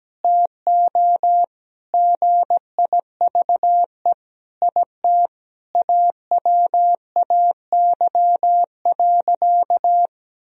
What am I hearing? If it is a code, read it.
Morse code, heard as a keyed tone, about 17 wpm, 700 Hz. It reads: TO GIVE IT AWAY.
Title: Sentences from Top 1000 Words - Review Part 1 - 17wpm